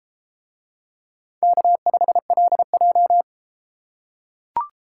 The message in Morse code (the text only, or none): K5LJ